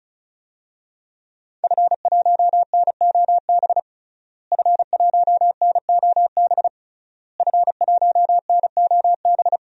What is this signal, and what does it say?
Morse code, heard as a keyed tone, about 36 wpm, 700 Hz. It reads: F1NOB F1NOB F1NOB
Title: DXCC-100 Call Signs - 5 Characters - Repeated 3x - 35wpm